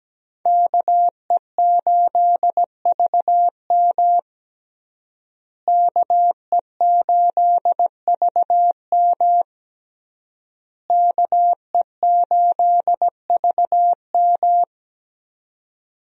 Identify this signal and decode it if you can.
Morse code, heard as a keyed tone, about 17 wpm, 700 Hz. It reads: KE8VM KE8VM KE8VM
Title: USA Call Signs - 5 Characters - Repeated 3x - 17wpm